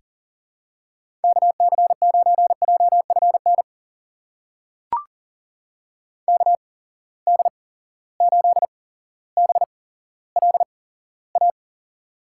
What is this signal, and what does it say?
Morse code, heard as a keyed tone, about 40 wpm, spaced as for 17 wpm, 700 Hz. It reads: KC9JFN KD8BLA
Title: USA Call Signs - 6 Characters - 8x Character Spacing - 40wpm